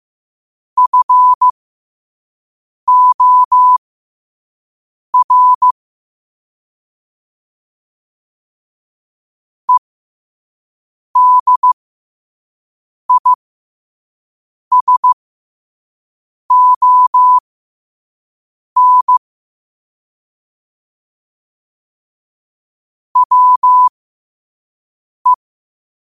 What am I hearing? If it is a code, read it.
Morse code, heard as a keyed tone, about 15 wpm, spaced as for 5 wpm, 1000 Hz. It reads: FOR EDISON WE
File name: qotdfiles / 5wpm